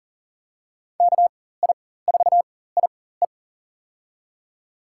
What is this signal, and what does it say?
Morse code, heard as a keyed tone, about 38 wpm, spaced as for 18 wpm, 700 Hz. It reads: KI4IE
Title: USA Call Signs - 5 Characters - 4x Character Spacing - 40wpm